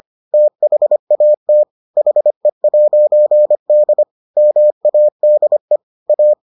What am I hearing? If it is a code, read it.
Morse code, heard as a keyed tone, about 25 wpm, 600 Hz. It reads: THAT HE'D MADE A